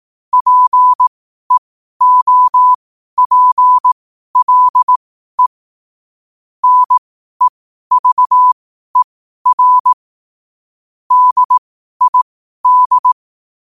Morse code, heard as a keyed tone, about 18 wpm, spaced as for 12 wpm, 1000 Hz. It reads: PEOPLE NEVER DID
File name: qotdfiles / 12wpm